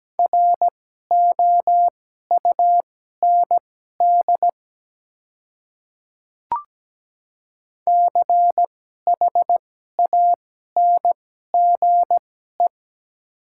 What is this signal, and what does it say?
Morse code, heard as a keyed tone, about 17 wpm, spaced as for 12 wpm, 700 Hz. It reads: ROUND CHANGE
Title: 2x Character Spacing - Top 300 Words - 17wpm